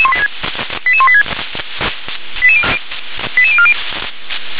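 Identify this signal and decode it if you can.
Five-tone selective call (ZVEI), heard as quick runs of five tones, tones of about 70 ms. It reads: 88186 88040
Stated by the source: ZVEI five-tone selective call received on SDR.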